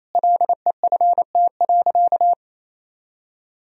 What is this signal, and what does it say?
Morse code, heard as a keyed tone, about 28 wpm, 700 Hz. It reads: LEFT.
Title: Sentences from Top 1000 Words - Review Part 1 - 28wpm